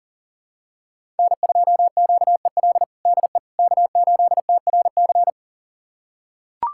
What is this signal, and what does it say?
Morse code, heard as a keyed tone, about 40 wpm, 700 Hz. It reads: N2QEL DE K8TRC
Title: USA Callsign DE Callsign - 5 Characters - 40wpm